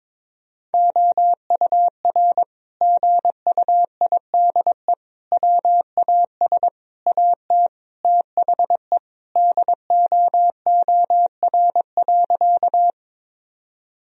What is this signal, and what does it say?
Morse code, heard as a keyed tone, about 21 wpm, 700 Hz. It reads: OUR GUIDE WAS AT THE DOOR.
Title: Sentences from Top 1000 Words - Review Part 3 - 22wpm